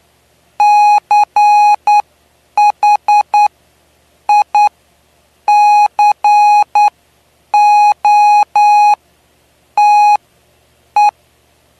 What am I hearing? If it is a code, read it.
Morse code, heard as a keyed tone, about 9 wpm, spaced as for 7 wpm, 830 Hz. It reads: CHICOTE
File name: cw / py2lcd / 09-PY2LCD-Li��o 9